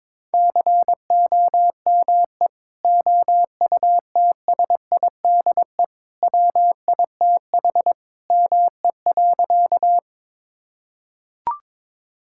Morse code, heard as a keyed tone, about 22 wpm, 700 Hz. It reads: COME OUTSIDE WITH ME.
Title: Sentences from Top 1000 Words - Review Part 6 - 22wpm